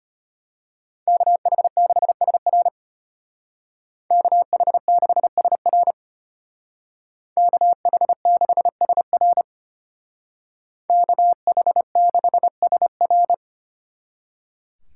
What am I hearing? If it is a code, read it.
Morse code, heard as a keyed tone, about 32 wpm, 700 Hz. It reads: KH6SR KH6SR KH6SR KH6SR